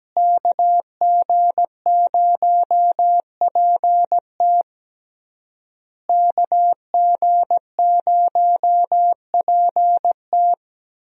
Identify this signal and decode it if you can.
Morse code, heard as a keyed tone, about 17 wpm, 700 Hz. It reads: KG0PT KG0PT
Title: USA Call Signs - 5 Characters - Repeated 3x - 17wpm